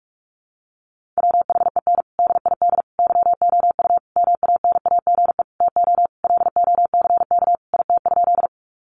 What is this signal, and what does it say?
Morse code, heard as a keyed tone, about 44 wpm, 700 Hz. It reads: WHEN DID YOU MANAGE TO LOCK IT?
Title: Sentences from Top 1400 Words 45wpm